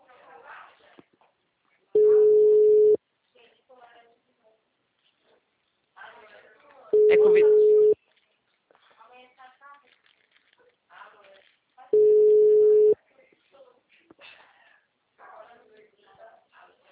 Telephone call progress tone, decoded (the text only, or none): ringback tone